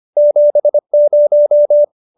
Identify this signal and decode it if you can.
Morse code, heard as a keyed tone, about 25 wpm, 590 Hz. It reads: 70